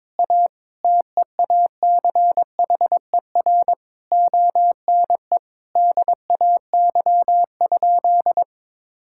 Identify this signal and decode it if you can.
Morse code, heard as a keyed tone, about 22 wpm, 700 Hz. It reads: A TEACHER ONE DAY?
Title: Sentences from Top 1000 Words - Review Part 6 - 22wpm